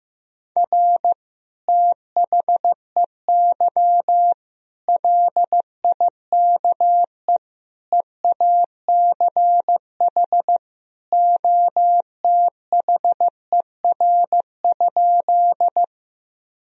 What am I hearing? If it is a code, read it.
Morse code, heard as a keyed tone, about 15 wpm, 700 Hz. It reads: R THEY LIKE EACH OTHER?
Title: Sentences from Top 200 Words and Abbreviations 15wpm